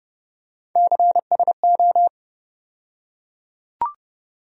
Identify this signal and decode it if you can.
Morse code, heard as a keyed tone, about 31 wpm, 700 Hz. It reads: CSO
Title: USA Call Sign Postfixes - 3 Letters 30wpm